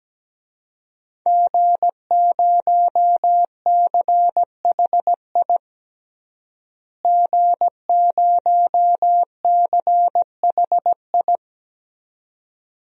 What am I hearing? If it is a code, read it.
Morse code, heard as a keyed tone, about 17 wpm, 700 Hz. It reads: G0CHI G0CHI
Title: DXCC-100 Call Signs - 5 Characters - Repeated 2x - 17wpm